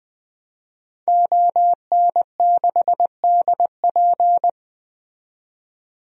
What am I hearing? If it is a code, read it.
Morse code, heard as a keyed tone, about 20 wpm, 700 Hz. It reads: ON6DP
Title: DXCC-100 Call Signs - 5 Characters - 20wpm